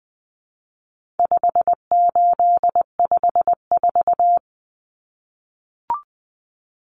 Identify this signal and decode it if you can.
Morse code, heard as a keyed tone, about 20 wpm, 700 Hz. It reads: 5854
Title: Common Cut Numbers - 4-Digit 20wpm